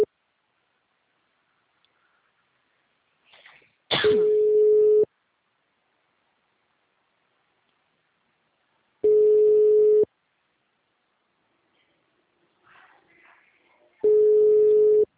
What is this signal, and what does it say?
Ringback tone (Europe, 425 Hz).